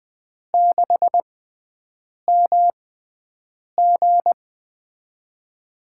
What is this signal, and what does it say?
Morse code, heard as a keyed tone, about 20 wpm, spaced as for 7 wpm, 700 Hz. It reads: 6MG
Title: USA Call Signs - 5 Characters - 6x Character Spacing - 20wpm